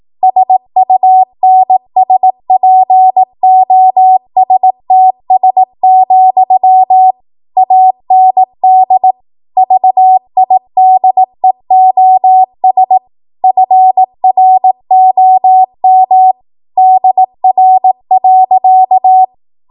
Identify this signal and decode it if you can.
Morse code, heard as a keyed tone, about 18 wpm, 750 Hz. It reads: SUNSPOSTS, AND VIDEOS FROM DR.
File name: Archive / 18 WPM / 18WPM